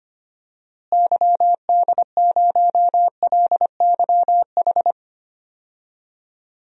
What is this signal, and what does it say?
Morse code, heard as a keyed tone, about 25 wpm, 700 Hz. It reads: YD0LYH
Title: DXCC-100 Call Signs - 6 Characters - Repeated 2x - 25wpm